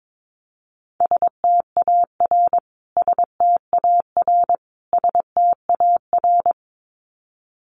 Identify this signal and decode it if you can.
Morse code, heard as a keyed tone, about 22 wpm, 700 Hz. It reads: STAR STAR STAR
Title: Top 500 Words Repeated 22wpm